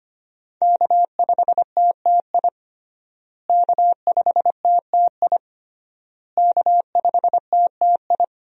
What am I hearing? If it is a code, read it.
Morse code, heard as a keyed tone, about 25 wpm, 700 Hz. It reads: K5TTI K5TTI K5TTI
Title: USA Call Signs - 5 Characters - Repeated 3x - 25wpm